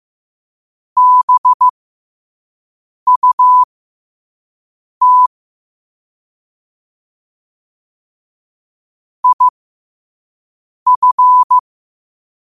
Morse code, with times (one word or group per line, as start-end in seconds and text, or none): 1.0-5.3 BUT
9.2-11.6 IF